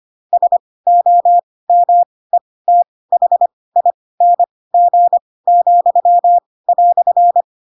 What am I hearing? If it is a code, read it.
Morse code, heard as a keyed tone, about 25 wpm, spaced as for 18 wpm, 700 Hz. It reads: SOMETHING,"